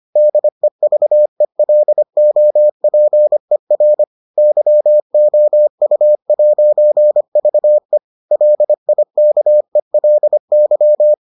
Morse code, heard as a keyed tone, about 25 wpm, 600 Hz. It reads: DEVELOPER YOU'VE LIKELY